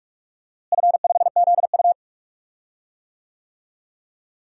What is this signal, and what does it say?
Morse code, heard as a keyed tone, about 43 wpm, 700 Hz. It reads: F5ZU
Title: DXCC-100 Call Signs - 4 Characters - Repeated 3x - 45wpm